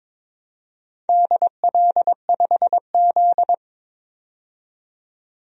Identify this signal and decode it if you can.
Morse code, heard as a keyed tone, about 22 wpm, 700 Hz. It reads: DL5Z
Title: DXCC-100 Call Signs - 4 Characters - 22wpm